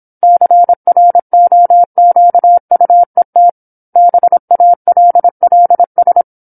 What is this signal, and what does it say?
Morse code, heard as a keyed tone, about 27 wpm, 700 Hz. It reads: CROQUET BALLS
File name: 08-Chapter-08 26 WPM